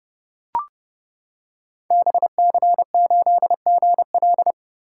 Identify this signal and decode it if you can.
Morse code, heard as a keyed tone, about 30 wpm, 700 Hz. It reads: BC8GL
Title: Code Groups - 5 Letter:Numbers - 30wpm